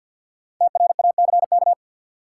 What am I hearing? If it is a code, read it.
Morse code, heard as a keyed tone, about 47 wpm, 700 Hz. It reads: TRACK
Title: Sentences from Top 1100 Words - New Word Drill - 50wpm